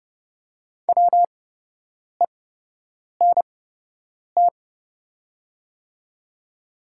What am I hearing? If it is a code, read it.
Morse code, heard as a keyed tone, about 30 wpm, spaced as for 8 wpm, 700 Hz. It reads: WENT
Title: 8x Character Spacing - Top 300 Words - 30wpm